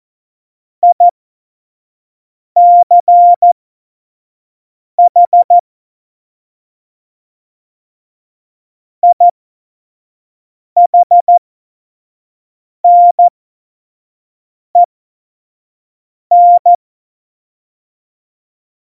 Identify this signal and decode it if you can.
Morse code, heard as a keyed tone, about 14 wpm, spaced as for 5 wpm, 700 Hz. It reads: ICH IHNEN
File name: Morsekurs / Witze / Witz 27 March 2018, 25 F70